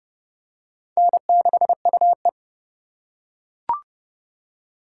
Morse code, heard as a keyed tone, about 30 wpm, 700 Hz. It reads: N6UE
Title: USA Call Signs - 4 Characters - 3x Character Spacing - 30wpm